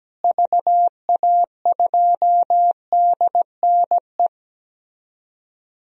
Morse code, heard as a keyed tone, about 17 wpm, 700 Hz. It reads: VA2DNE
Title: DXCC-100 Call Signs - 6 Characters - Repeated 2x - 17wpm